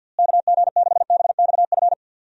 Morse code, heard as a keyed tone, about 50 wpm, 700 Hz. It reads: KG6BXF
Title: USA Call Signs - 6 Characters - Repeated 3x - 50wpm